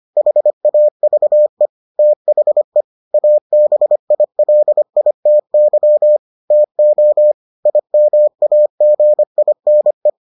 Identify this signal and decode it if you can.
Morse code, heard as a keyed tone, about 25 wpm, 600 Hz. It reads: HAVE THE ABILITY TO IMAGINE